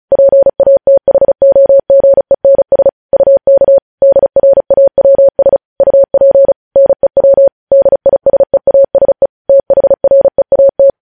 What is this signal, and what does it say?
Morse code, heard as a keyed tone, about 36 wpm, 550 Hz. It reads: PATHOGENS UK DRAWS UP NEW DISEASE THREAT
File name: podcast / News headlines-1x-35x35